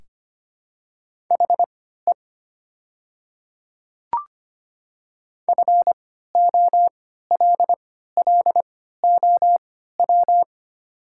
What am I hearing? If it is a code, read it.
Morse code, heard as a keyed tone, about 25 wpm, spaced as for 14 wpm, 700 Hz. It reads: HE FOLLOW
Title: 3x Character Spacing - Top 300 Words - 25wpm